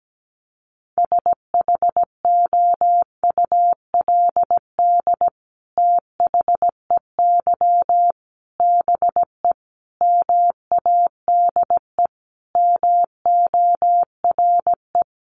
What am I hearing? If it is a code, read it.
Morse code, heard as a keyed tone, about 17 wpm, 700 Hz. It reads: SHOULD THEY BE MADE MORE
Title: Sentences from Top 1400 Words 17wpm